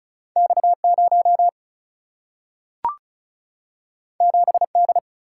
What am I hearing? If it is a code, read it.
Morse code, heard as a keyed tone, about 36 wpm, 700 Hz. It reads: X 0 7 D
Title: Single-Letter Single-Number 35wpm